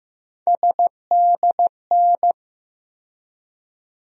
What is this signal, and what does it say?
Morse code, heard as a keyed tone, about 15 wpm, 700 Hz. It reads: SDN
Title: USA Call Sign Postfixes - 3 Letters 15wpm